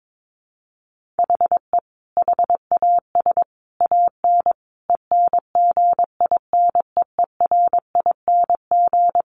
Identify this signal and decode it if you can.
Morse code, heard as a keyed tone, about 22 wpm, 700 Hz. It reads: HE HAS AN ENGINEERING
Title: Sentences from Top 1500 Words 22wpm